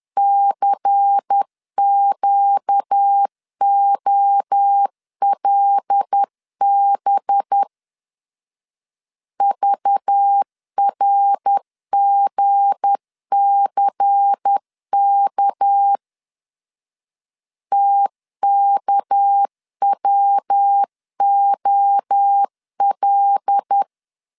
Morse code, heard as a keyed tone, about 10 wpm, 800 Hz. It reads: CQOLB VRGCK TKWOL
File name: LEZIONI / 7 GF9Z / 7 G 10wpm 2501